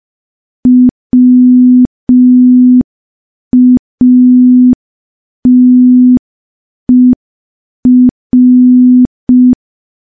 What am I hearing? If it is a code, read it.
Morse code, heard as a keyed tone, about 5 wpm, 250 Hz. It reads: WATER
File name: morse / morse-training-05WPM